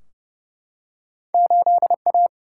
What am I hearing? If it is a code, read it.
Morse code, heard as a keyed tone, about 30 wpm, 700 Hz. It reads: 8A
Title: Single-Letter Single-Number 30wpm